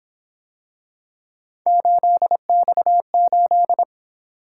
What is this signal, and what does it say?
Morse code, heard as a keyed tone, about 26 wpm, 700 Hz. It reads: 8X8
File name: Speed-Racing - Three Letter-Number Combinations - 22wpm